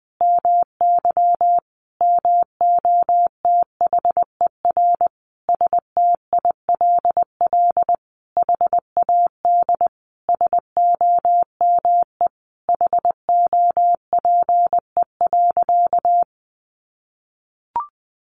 Morse code, heard as a keyed tone, about 20 wpm, 700 Hz. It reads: MY MOTHER STILL HAD SOME HOPE.
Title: Sentences from Top 1000 Words 20wpm